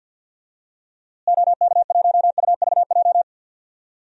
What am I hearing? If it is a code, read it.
Morse code, heard as a keyed tone, about 46 wpm, 700 Hz. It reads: OK1UVJ
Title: DXCC-100 Call Signs - 6 Characters - Repeated 2x - 50wpm